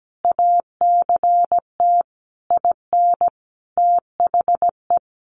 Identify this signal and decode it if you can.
Morse code, heard as a keyed tone, about 17 wpm, 700 Hz. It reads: ACT IN THE
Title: Sentences from Top 1500 Words 17wpm